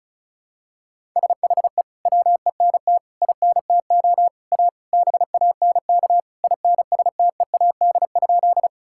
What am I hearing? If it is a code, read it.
Morse code, heard as a keyed tone, about 34 wpm, 700 Hz. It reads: SHE WENT INTO A BANK INSTEAD?
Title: Sentences from Top 1000 Words - Review Part 6 - 35wpm